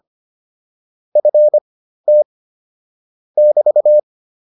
Morse code, heard as a keyed tone, about 25 wpm, spaced as for 13 wpm, 600 Hz. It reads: FT <BT>